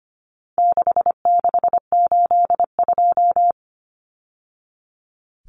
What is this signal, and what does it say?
Morse code, heard as a keyed tone, about 25 wpm, 700 Hz. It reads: 6682
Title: Numbers - 4-Digit 25wpm